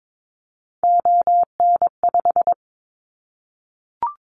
Morse code, heard as a keyed tone, about 22 wpm, 700 Hz. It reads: ON5
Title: Callsign Prefixes for Top 25 DXCC - 22wpm